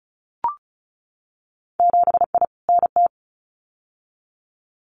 Morse code, heard as a keyed tone, about 36 wpm, 700 Hz. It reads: 7I NT
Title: Winter Field Day Exchanges Encore 35wpm